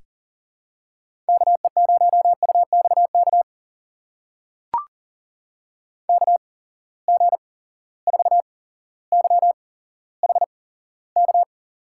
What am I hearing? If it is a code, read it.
Morse code, heard as a keyed tone, about 39 wpm, spaced as for 17 wpm, 700 Hz. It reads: KE0UXK KG4YHK
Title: USA Call Signs - 6 Characters - 8x Character Spacing - 40wpm